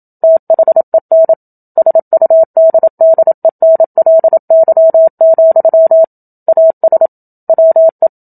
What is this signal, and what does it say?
Morse code, heard as a keyed tone, about 27 wpm, 650 Hz. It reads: THEN SUDDENLY, AS WE
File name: Book Two / 01-Chapter 1 - 27 WPM